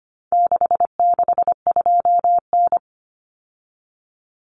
Morse code, heard as a keyed tone, about 25 wpm, 700 Hz. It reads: 662N
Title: Common Cut Numbers - 4-Digit 25wpm